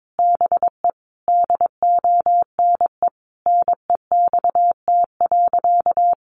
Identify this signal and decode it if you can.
Morse code, heard as a keyed tone, about 22 wpm, 700 Hz. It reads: BE DONE NEXT.
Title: Sentences from Top 1200 Words 22wpm